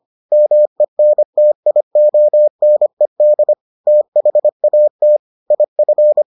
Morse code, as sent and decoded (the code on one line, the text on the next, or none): -- . -. - .. --- -. -.. / - .... .- - / .. ..-.
MENTIONED THAT IF